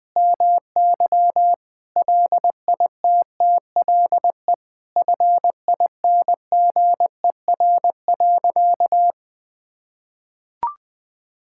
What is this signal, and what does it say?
Morse code, heard as a keyed tone, about 20 wpm, 700 Hz. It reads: MY LITTLE FINGER.
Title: Sentences from Top 1600 Words 20wpm